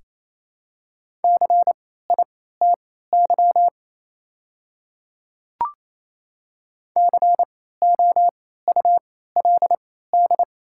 Morse code, heard as a keyed tone, about 28 wpm, spaced as for 16 wpm, 700 Hz. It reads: CITY COULD
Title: 3x Character Spacing - Top 300 Words - 28wpm